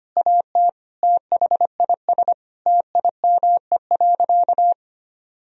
Morse code, heard as a keyed tone, about 25 wpm, 700 Hz. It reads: AT THIS TIME.
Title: Sentences from Top 1000 Words - Review Part 3 - 25wpm